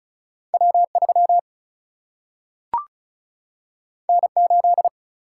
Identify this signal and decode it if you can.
Morse code, heard as a keyed tone, about 36 wpm, 700 Hz. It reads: W 3 N 8
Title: Single-Letter Single-Number 35wpm